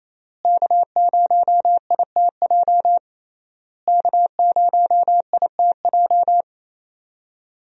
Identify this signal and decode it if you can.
Morse code, heard as a keyed tone, about 27 wpm, 700 Hz. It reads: K0ITJ K0ITJ
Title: USA Call Signs - 5 Characters - Repeated 3x - 28wpm